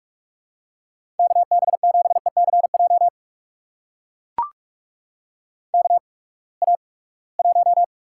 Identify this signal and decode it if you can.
Morse code, heard as a keyed tone, about 44 wpm, spaced as for 25 wpm, 700 Hz. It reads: KB7ECJ KA1
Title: USA Call Signs - 6 Characters - 8x Character Spacing - 45wpm